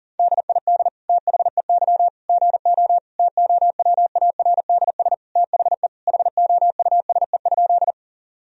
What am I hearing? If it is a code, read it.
Morse code, heard as a keyed tone, about 37 wpm, 700 Hz. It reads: DID THEY GO TOWARDS THE HOUSE?
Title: Sentences from Top 1000 Words - Review Part 2 - 40wpm